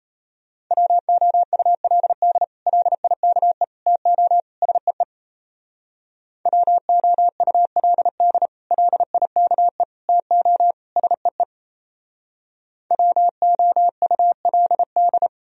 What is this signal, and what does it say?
Morse code, heard as a keyed tone, about 36 wpm, 700 Hz. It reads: WOULD LIKE TO SEE WOULD LIKE TO SEE WOULD